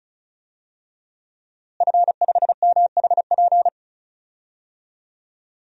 Morse code, heard as a keyed tone, about 34 wpm, 700 Hz. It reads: F5MHP
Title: DXCC-100 Call Signs - 5 Characters - Repeated 2x - 35wpm